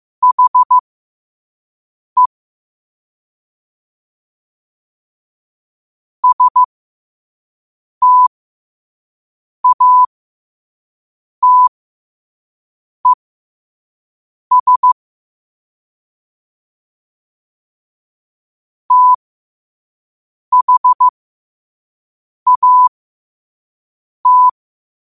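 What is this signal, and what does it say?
Morse code, heard as a keyed tone, about 15 wpm, spaced as for 5 wpm, 1000 Hz. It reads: HE STATES THAT
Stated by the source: Morse at 5 wpm.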